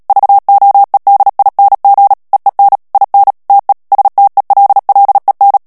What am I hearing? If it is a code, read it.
Morse code, heard as a keyed tone, about 37 wpm, 800 Hz. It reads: VOEDING EEN IN TE STELLEN
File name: morse mp3 / HOLLANDAIS texte clair 2 vitesse 35 mots minute